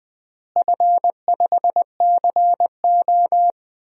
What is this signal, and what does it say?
Morse code, heard as a keyed tone, about 20 wpm, 700 Hz. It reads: F5CO
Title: DXCC-100 Call Signs - 4 Characters - Repeated 2x - 20wpm